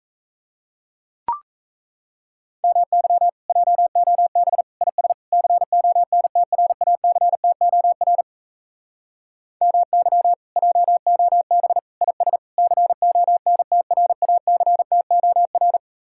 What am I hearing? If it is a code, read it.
Morse code, heard as a keyed tone, about 36 wpm, 700 Hz. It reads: MY JOB IS CONTRACTOR MY JOB IS CONTRACTOR